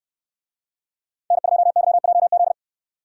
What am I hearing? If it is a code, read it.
Morse code, heard as a keyed tone, about 65 wpm, 700 Hz. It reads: N2YJZ